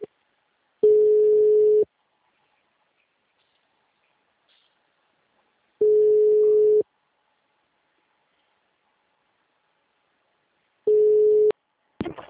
Ringback tone (Europe, 425 Hz).
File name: out- -2002- - - .21